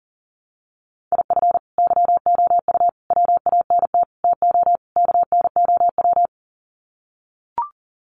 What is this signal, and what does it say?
Morse code, heard as a keyed tone, about 40 wpm, 700 Hz. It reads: IF YOU WANT TO KNOW
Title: Sets of 5 Words Encore2 40wpm